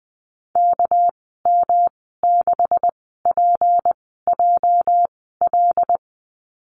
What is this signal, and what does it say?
Morse code, heard as a keyed tone, about 20 wpm, spaced as for 14 wpm, 700 Hz. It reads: KM6PJL
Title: 207 Lesson 33g - Letter j - Call Signs 20wpm